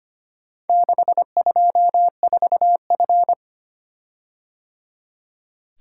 Morse code, heard as a keyed tone, about 25 wpm, 700 Hz. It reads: 624F
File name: practice / 25wpm / Four Letter-Number Combinations - 25wpm